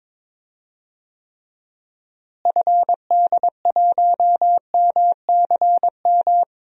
Morse code, heard as a keyed tone, about 21 wpm, 700 Hz. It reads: FD1MCM
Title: DXCC-100 Call Signs - 6 Characters - Repeated 3x - 22wpm